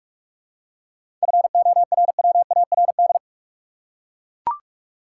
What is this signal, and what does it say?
Morse code, heard as a keyed tone, about 41 wpm, 700 Hz. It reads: FORWARD
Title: Seven Letter Words - 45wpm